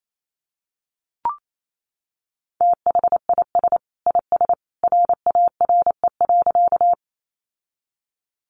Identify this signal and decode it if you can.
Morse code, heard as a keyed tone, about 28 wpm, 700 Hz. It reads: THIS IS RARE.
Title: Sentences from Top 1100 Words 28wpm